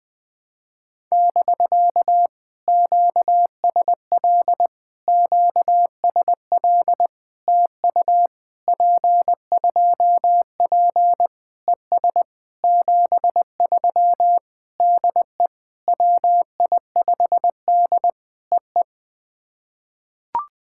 Morse code, heard as a keyed tone, about 20 wpm, 700 Hz. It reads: <BK> QSL QSL TU P2P ES 73 DE WI5D EE
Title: POTA QSO Practice - 20wpm